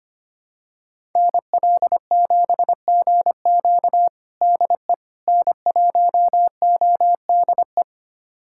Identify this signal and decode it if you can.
Morse code, heard as a keyed tone, about 25 wpm, 700 Hz. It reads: NL7GQ DE N1ODE
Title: USA Callsign DE Callsign - 5 Characters - 25wpm